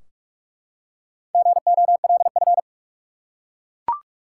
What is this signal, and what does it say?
Morse code, heard as a keyed tone, about 43 wpm, 700 Hz. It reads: GOLF
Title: Four Letter Words - 45wpm